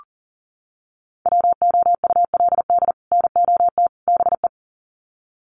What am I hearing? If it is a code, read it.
Morse code, heard as a keyed tone, about 40 wpm, 700 Hz. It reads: WOULD NOT BE
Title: Sets of 3 Words 40wpm